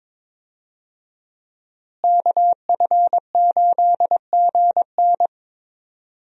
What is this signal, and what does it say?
Morse code, heard as a keyed tone, about 22 wpm, 700 Hz. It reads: KF8GN
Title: USA Call Signs - 5 Characters - Repeated 2x - 22wpm